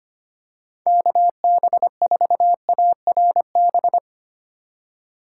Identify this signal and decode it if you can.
Morse code, heard as a keyed tone, about 25 wpm, 700 Hz. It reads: KB4ARB
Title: USA Call Signs - 6 Characters - 2x Character Spacing - 25wpm